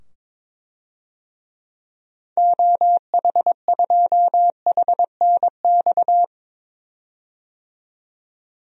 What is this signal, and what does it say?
Morse code, heard as a keyed tone, about 22 wpm, 700 Hz. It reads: OH2HNX
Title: DXCC-100 Call Signs - 6 Characters - 22wpm